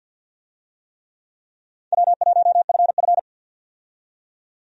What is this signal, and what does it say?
Morse code, heard as a keyed tone, about 52 wpm, 700 Hz. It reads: W1FF
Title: USA Call Signs - 4 Characters - 50wpm